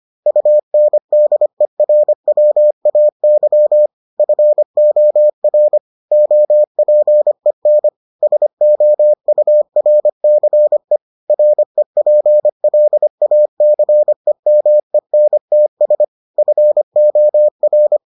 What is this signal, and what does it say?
Morse code, heard as a keyed tone, about 25 wpm, 600 Hz. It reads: UNDERWAY FOR OPEN SOURCE REPLACEMENTS FOR